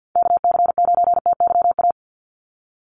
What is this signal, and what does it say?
Morse code, heard as a keyed tone, about 49 wpm, 700 Hz. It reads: KC9TYA
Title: NA SOTA Chaser Callsigns 50wpm